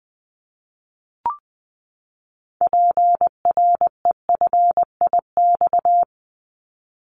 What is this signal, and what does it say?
Morse code, heard as a keyed tone, about 20 wpm, 700 Hz. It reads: PREFIX
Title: Callsign Prefixes for Top 100 DXCC - 20wpm